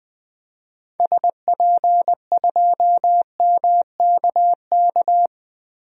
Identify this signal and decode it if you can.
Morse code, heard as a keyed tone, about 20 wpm, 700 Hz. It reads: SP2MKK
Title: DXCC-100 Call Signs - 6 Characters - Repeated 3x - 20wpm